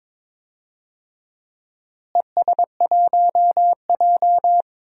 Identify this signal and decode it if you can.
Morse code, heard as a keyed tone, about 22 wpm, 700 Hz. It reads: ES1J
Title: DXCC-100 Call Signs - 4 Characters - Repeated 3x - 22wpm